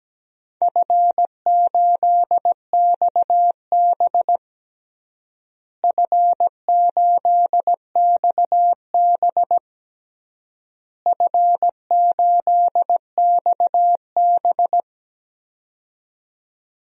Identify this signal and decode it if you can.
Morse code, heard as a keyed tone, about 17 wpm, 700 Hz. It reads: F8XB F8XB F8XB
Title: DXCC-100 Call Signs - 4 Characters - Repeated 3x - 17wpm